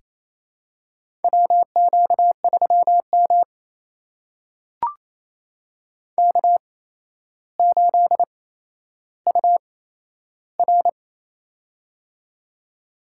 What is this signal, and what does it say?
Morse code, heard as a keyed tone, about 28 wpm, spaced as for 12 wpm, 700 Hz. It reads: WQ3M K8UR
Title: USA Call Signs - 4 Characters - 8x Character Spacing - 28wpm